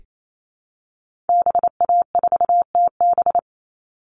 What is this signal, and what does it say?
Morse code, heard as a keyed tone, about 28 wpm, 700 Hz. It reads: BA4TB
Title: Call Signs Repeated 2x - 28wpm